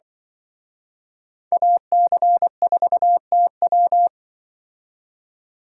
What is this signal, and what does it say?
Morse code, heard as a keyed tone, about 24 wpm, 700 Hz. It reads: AC4TW
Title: Speed Racing - USA 2x2 Call Signs 20wpm